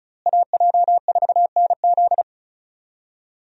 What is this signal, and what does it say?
Morse code, heard as a keyed tone, about 34 wpm, 700 Hz. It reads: AJ4NZ
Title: USA Call Signs - 5 Characters - Repeated 2x - 35wpm